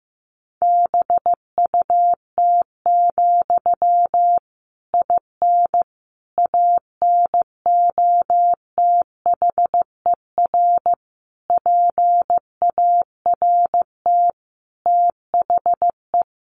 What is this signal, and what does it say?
Morse code, heard as a keyed tone, about 15 wpm, 700 Hz. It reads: BUT, IN ANOTHER PART THE
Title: Sentences from Top 1500 Words 15wpm